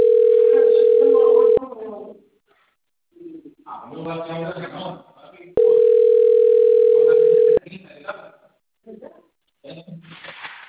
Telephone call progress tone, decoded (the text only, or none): ringback tone